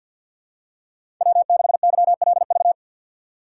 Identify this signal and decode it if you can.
Morse code, heard as a keyed tone, about 52 wpm, 700 Hz. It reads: W6YLV
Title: USA Call Signs - 4x Character Spacing - 50wpm